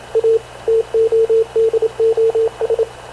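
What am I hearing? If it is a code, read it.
Morse code, heard as a keyed tone, about 27 wpm, 450 Hz, with radio noise. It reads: A TODOS
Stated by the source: Havana Radio CLA press (PX) broadcast, recorded by Radio Officer Paul Dumesnil aboard mv CNAV QUEST/CZDO.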